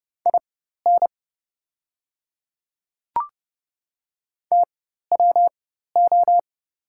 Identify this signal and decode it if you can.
Morse code, heard as a keyed tone, about 30 wpm, spaced as for 14 wpm, 690 Hz. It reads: IN TWO
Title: 4x Character Spacing - Top 100 Words - 30wpm